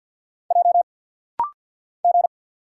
Morse code, heard as a keyed tone, about 50 wpm, 690 Hz. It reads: J G
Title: Single Letters - Rapid-Fire 50wpm